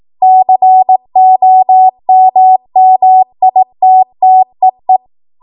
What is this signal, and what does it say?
Morse code, heard as a keyed tone, about 18 wpm, 750 Hz. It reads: COMMITTEE